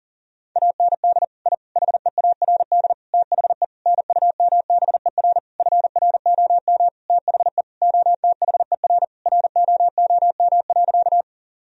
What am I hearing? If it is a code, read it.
Morse code, heard as a keyed tone, about 40 wpm, 700 Hz. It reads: AND I HEARD THE NUMBER FROM THE OTHER ROOM.
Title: Sentences from Top 1000 Words - Review Part 1 - 40wpm